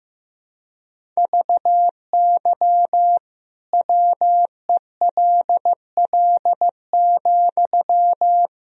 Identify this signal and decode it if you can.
Morse code, heard as a keyed tone, about 15 wpm, 700 Hz. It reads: VY WELL,
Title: Sentences from Top 200 Words and Abbreviations 15wpm